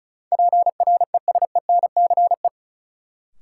Morse code, heard as a keyed tone, about 36 wpm, 700 Hz. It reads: PRESENCE
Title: Sets of 3 Words - New Word Drill - 35wpm